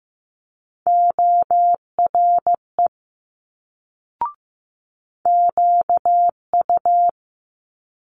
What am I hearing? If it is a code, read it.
Morse code, heard as a keyed tone, about 15 wpm, 700 Hz. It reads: ORE QU
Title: Multi-letter Phonemes Quick Fire 15wpm